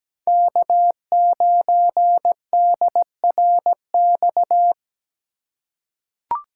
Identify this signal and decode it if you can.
Morse code, heard as a keyed tone, about 17 wpm, 700 Hz. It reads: K9DRX
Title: USA Call Signs - 3x Character Spacing - 17wpm